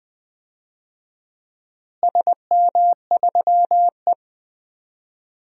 Morse code, heard as a keyed tone, about 20 wpm, 700 Hz. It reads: SM3E
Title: DXCC-100 Call Signs - 4 Characters - Repeated 3x - 20wpm